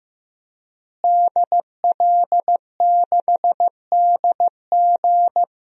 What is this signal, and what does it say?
Morse code, heard as a keyed tone, about 15 wpm, 700 Hz. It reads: DL6DG
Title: DXCC-100 Call Signs - 5 Characters - Repeated 3x - 15wpm